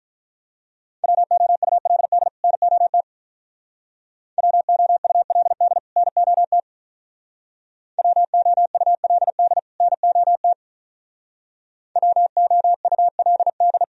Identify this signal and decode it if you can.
Morse code, heard as a keyed tone, about 44 wpm, 700 Hz. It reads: WOULD NOT WOULD NOT WOULD NOT WOULD